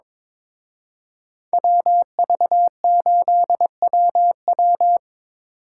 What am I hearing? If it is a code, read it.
Morse code, heard as a keyed tone, about 25 wpm, 700 Hz. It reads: WV8WW
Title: Speed Racing - USA 2x2 Call Signs 22wpm